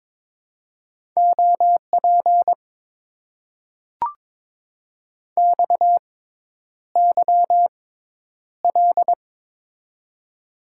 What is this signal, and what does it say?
Morse code, heard as a keyed tone, about 23 wpm, spaced as for 10 wpm, 700 Hz. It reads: OP XYL
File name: Top 90 QSO Elements - 6x Character Spacing - 22wpm